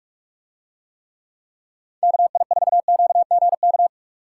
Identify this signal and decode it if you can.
Morse code, heard as a keyed tone, about 47 wpm, 700 Hz. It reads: KI4QGK